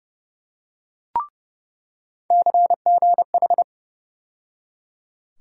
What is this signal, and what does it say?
Morse code, heard as a keyed tone, about 29 wpm, 700 Hz. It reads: CGH
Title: USA Call Sign Postfixes - 3 Letters 30wpm